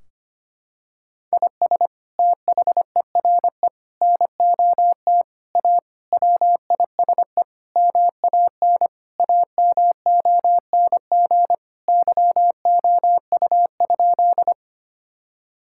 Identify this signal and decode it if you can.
Morse code, heard as a keyed tone, about 25 wpm, 700 Hz. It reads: IS THERE NOT A WISE MAN AMONG YOU?
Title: Sentences from Top 1600 Words 25wpm